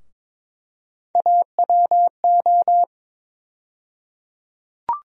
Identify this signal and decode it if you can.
Morse code, heard as a keyed tone, about 22 wpm, 700 Hz. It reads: AWO
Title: USA Call Sign Postfixes - 3 Letters 22wpm